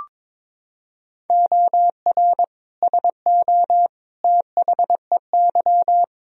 Morse code, heard as a keyed tone, about 22 wpm, 700 Hz. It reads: OR SO THEY
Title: Sentences from Top 100 Words and Abbreviations 22wpm